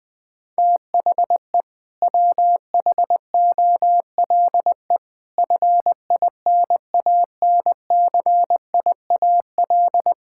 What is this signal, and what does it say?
Morse code, heard as a keyed tone, about 21 wpm, 700 Hz. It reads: THE WHOLE FINANCIAL
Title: Sentences from Top 1000 Words - Review Part 5 - 20wpm